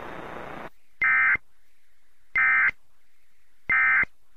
Emergency Alert System (SAME) tones, heard as harsh buzzing bursts. It is heard from 1.0 to 4.0 s.